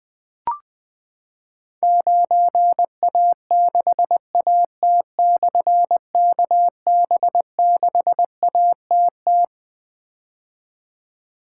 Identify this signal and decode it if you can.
Morse code, heard as a keyed tone, about 21 wpm, 700 Hz. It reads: 9A6AT/KB6ATT
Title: Difficult Call Signs 20wpm